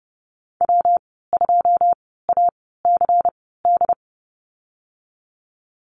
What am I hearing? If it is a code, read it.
Morse code, heard as a keyed tone, about 30 wpm, spaced as for 17 wpm, 700 Hz. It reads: W2ACD
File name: Lesson 19g - Letter w - Call Signs 30wpm